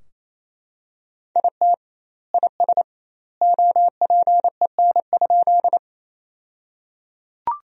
Morse code, heard as a keyed tone, about 27 wpm, 700 Hz. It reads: IT IS OPEN?
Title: 2x Word-Spacing - Sentences from Top 300 Words - 28wpm